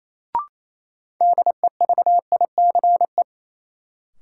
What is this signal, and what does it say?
Morse code, heard as a keyed tone, about 28 wpm, 700 Hz. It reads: DEVICE
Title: Sentences from Top 1500 Words - New Word Drill - 28wpm